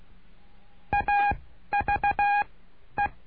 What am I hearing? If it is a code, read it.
Morse code, heard as a keyed tone, about 16 wpm, spaced as for 11 wpm, 820 Hz. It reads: AVE
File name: cursos / CURSO DE TELEGRAFIA / Telegrafia - Trilha 09